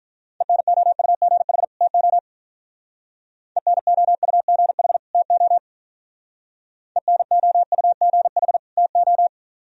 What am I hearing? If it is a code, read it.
Morse code, heard as a keyed tone, about 43 wpm, 700 Hz. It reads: ENOUGH TO ENOUGH TO ENOUGH TO
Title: Speed-Racing - Sets of 2 Words 35wpm